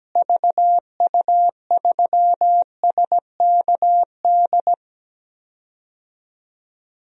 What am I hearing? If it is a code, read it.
Morse code, heard as a keyed tone, about 17 wpm, 700 Hz. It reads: VU3SKD
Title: DXCC-100 Call Signs - 6 Characters - Repeated 2x - 17wpm